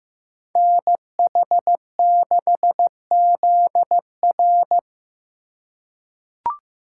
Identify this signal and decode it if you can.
Morse code, heard as a keyed tone, about 15 wpm, 700 Hz. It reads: NH6ZR
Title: USA Call Signs - 5 Characters - 2x Character Spacing - 15wpm